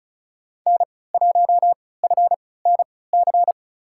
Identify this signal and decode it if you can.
Morse code, heard as a keyed tone, about 36 wpm, spaced as for 20 wpm, 700 Hz. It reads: N1FNC
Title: USA Call Signs - 5 Characters - 3x Character Spacing - 35wpm